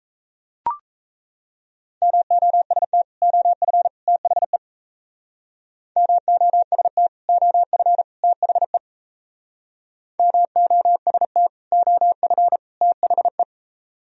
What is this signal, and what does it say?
Morse code, heard as a keyed tone, about 34 wpm, 700 Hz. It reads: MOST OF THE MOST OF THE MOST OF THE
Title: Speed-Racing - Sets of 3 Words 28wpm